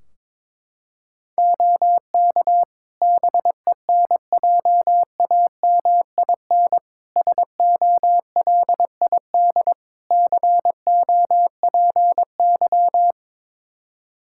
Morse code, heard as a keyed tone, about 22 wpm, 700 Hz. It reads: OK BENJAMIN SOLID COPY
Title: Practice QSO - Exchange 3 - More Info - 22wpm